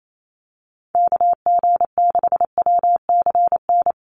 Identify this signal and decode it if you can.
Morse code, heard as a keyed tone, about 28 wpm, 700 Hz. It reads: KG6WCN
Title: USA Call Signs Repeated 3x 28wpm